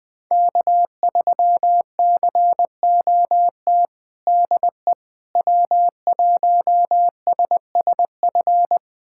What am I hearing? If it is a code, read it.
Morse code, heard as a keyed tone, about 20 wpm, 700 Hz. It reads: K3COT DE W1SSF
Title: USA Callsign DE Callsign 20wpm